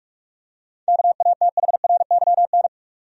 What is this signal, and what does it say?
Morse code, heard as a keyed tone, about 46 wpm, 700 Hz. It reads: KATHRYN
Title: USA Common Names - 45wpm